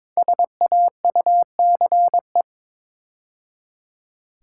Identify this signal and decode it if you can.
Morse code, heard as a keyed tone, about 22 wpm, 700 Hz. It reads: SAUCE
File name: 5 Letter Words - First 3 Letters Given - 22wpm